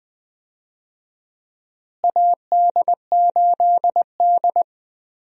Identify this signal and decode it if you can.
Morse code, heard as a keyed tone, about 20 wpm, 700 Hz. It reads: AD8D
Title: USA Call Signs - 4 Characters - Repeated 2x - 20wpm